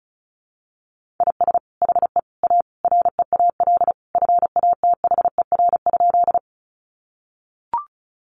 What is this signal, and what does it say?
Morse code, heard as a keyed tone, about 34 wpm, 700 Hz. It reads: IS HE A REAL FATHER?
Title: Sentences from Top 300 Words 35wpm